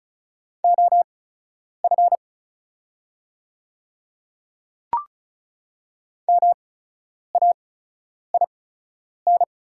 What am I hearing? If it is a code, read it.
Morse code, heard as a keyed tone, about 36 wpm, spaced as for 10 wpm, 700 Hz. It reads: OF MAIN
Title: 8x Character Spacing - Top 300 Words - 35wpm